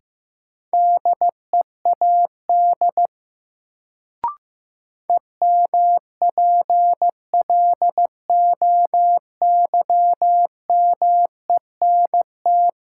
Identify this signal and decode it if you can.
Morse code, heard as a keyed tone, about 15 wpm, 700 Hz. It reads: DEAD EMPLOYMENT
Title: Sentences from Top 1500 Words - New Word Drill - 15wpm